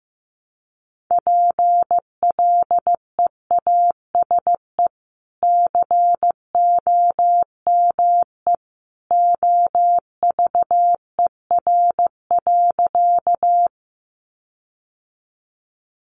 Morse code, heard as a keyed tone, about 15 wpm, 700 Hz. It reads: PLEASE COME OVER.
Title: Sentences from Top 1500 Words 15wpm